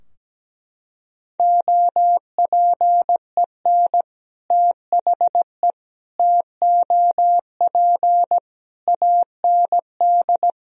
Morse code, heard as a keyed tone, about 17 wpm, 700 Hz. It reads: OPEN THE TOP AND
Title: Sentences from Top 1000 Words - Review Part 2 - 17wpm